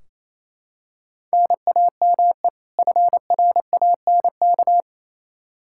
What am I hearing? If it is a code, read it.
Morse code, heard as a keyed tone, about 28 wpm, 700 Hz. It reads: NAME FRANK
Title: Practice QSO - Exchange 1 - Beginning - 28wpm